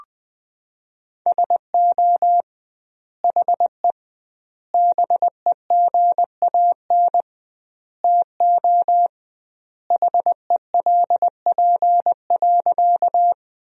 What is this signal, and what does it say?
Morse code, heard as a keyed tone, about 21 wpm, 700 Hz. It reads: SO HE BEGAN TO HELP.
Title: 2x Word-Spacing - Sentences from Top 300 Words - 20wpm